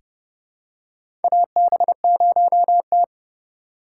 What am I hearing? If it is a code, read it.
Morse code, heard as a keyed tone, about 31 wpm, 700 Hz. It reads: AB0T
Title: USA Call Signs - 3x Character Spacing - 30wpm